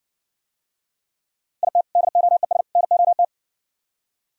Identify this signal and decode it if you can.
Morse code, heard as a keyed tone, about 59 wpm, 700 Hz. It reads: IT DOES NOT